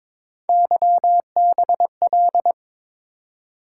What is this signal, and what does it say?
Morse code, heard as a keyed tone, about 22 wpm, 700 Hz. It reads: YBL
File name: practice / 22wpm / Three Letter-Number Combinations - 22wpm